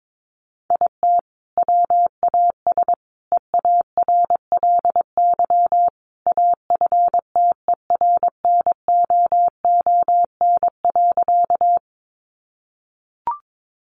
Morse code, heard as a keyed tone, about 22 wpm, 700 Hz. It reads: IT WAS EARLY AFTERNOON.
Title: Sentences from Top 1500 Words 22wpm